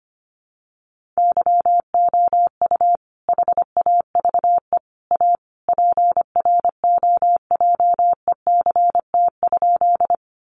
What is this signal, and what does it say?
Morse code, heard as a keyed tone, about 25 wpm, 700 Hz. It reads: YOU HAVE A PROJECT?
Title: Sentences from Top 1400 Words 25wpm